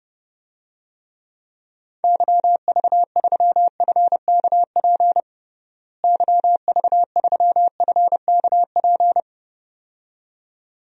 Morse code, heard as a keyed tone, about 30 wpm, 700 Hz. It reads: YV3FKP YV3FKP
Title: DXCC-100 Call Signs - 6 Characters - Repeated 2x - 30wpm